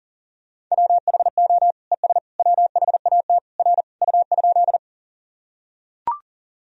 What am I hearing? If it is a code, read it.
Morse code, heard as a keyed tone, about 40 wpm, 700 Hz. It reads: WHO ES WHAT R U?
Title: Sentences from Top 100 Words and Abbreviations 40wpm